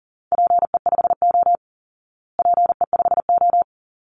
Morse code, heard as a keyed tone, about 40 wpm, 700 Hz. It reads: PE5O PE5O
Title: Call Signs Repeated 3x - 40wpm